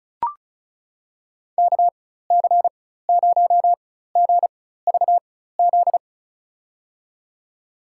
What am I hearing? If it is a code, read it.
Morse code, heard as a keyed tone, about 36 wpm, spaced as for 16 wpm, 700 Hz. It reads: KC0GVZ
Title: USA Call Signs - 6 Characters - 4x Character Spacing - 35wpm